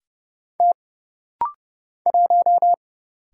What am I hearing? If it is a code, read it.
Morse code, heard as a keyed tone, about 30 wpm, 700 Hz. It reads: T 1
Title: Single Letter-Number Alt - Rapid-Fire 30wpm